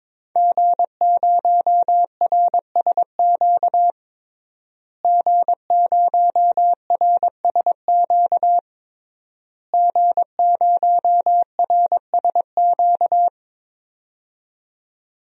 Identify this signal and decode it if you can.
Morse code, heard as a keyed tone, about 22 wpm, 700 Hz. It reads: G0RSQ G0RSQ G0RSQ
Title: DXCC-100 Call Signs - 5 Characters - Repeated 3x - 22wpm